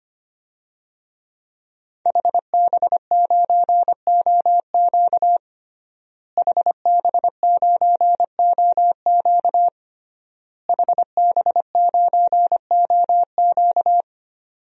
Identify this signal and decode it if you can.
Morse code, heard as a keyed tone, about 25 wpm, 700 Hz. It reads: HB9OQ HB9OQ HB9OQ
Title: DXCC-100 Call Signs - 5 Characters - Repeated 3x - 25wpm